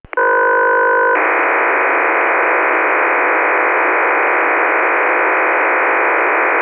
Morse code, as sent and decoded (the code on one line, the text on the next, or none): none